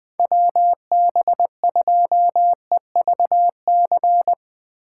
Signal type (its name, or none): Morse code